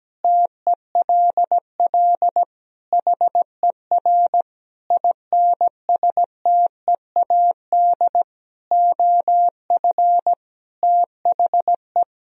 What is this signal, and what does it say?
Morse code, heard as a keyed tone, about 17 wpm, 700 Hz. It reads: TELL HER INSTEAD OF THE
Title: Sentences from Top 1000 Words - Review Part 4 - 17wpm